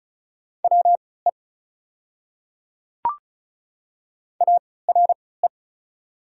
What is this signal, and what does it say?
Morse code, heard as a keyed tone, about 33 wpm, spaced as for 19 wpm, 690 Hz. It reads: WE ARE
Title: 3x Character Spacing - Top 100 Words - 35wpm